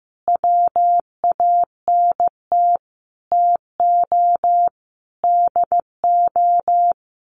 Morse code, heard as a keyed tone, about 15 wpm, 700 Hz. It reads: WANT TO DO
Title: Sets of 5 Words 15wpm